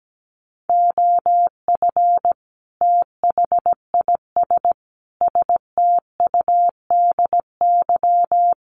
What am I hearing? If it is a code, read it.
Morse code, heard as a keyed tone, about 17 wpm, 700 Hz. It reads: OF THIS STUDY
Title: Sets of 5 Words 17wpm